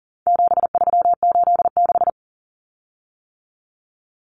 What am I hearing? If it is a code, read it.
Morse code, heard as a keyed tone, about 40 wpm, 700 Hz. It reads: 7386
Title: Common Cut Numbers - 4-Digit 40wpm